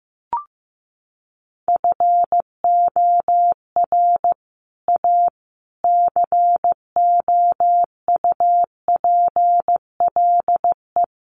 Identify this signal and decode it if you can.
Morse code, heard as a keyed tone, about 15 wpm, 700 Hz. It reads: FOR A COUPLE
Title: Sets of 5 Words Encore2 15wpm